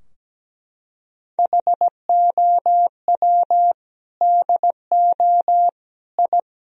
Morse code, heard as a keyed tone, about 17 wpm, 700 Hz. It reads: HOW DO I
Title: Sentences from Top 1000 Words - Review Part 2 - 17wpm